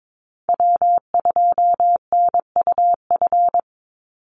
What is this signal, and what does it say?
Morse code, heard as a keyed tone, about 22 wpm, 700 Hz. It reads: W2NUF
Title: USA Call Signs 22wpm